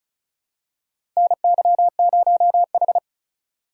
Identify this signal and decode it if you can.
Morse code, heard as a keyed tone, about 36 wpm, 700 Hz. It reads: NY0H
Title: USA Call Signs - 4 Characters - 4x Character Spacing - 35wpm